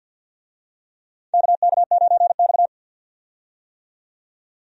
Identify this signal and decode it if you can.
Morse code, heard as a keyed tone, about 49 wpm, 700 Hz. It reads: KK9X
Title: USA Call Signs - 4 Characters - 50wpm